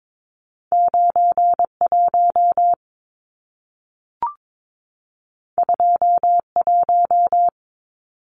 Morse code, heard as a keyed tone, about 22 wpm, 700 Hz. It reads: 91 21
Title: Numbers - 2-Digit 22wpm